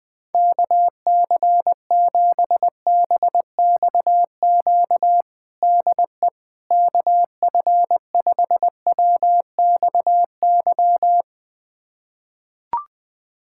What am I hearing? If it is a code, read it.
Morse code, heard as a keyed tone, about 20 wpm, 700 Hz. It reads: KC7BXQ DE KF5WXY
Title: USA Callsign DE Callsign 20wpm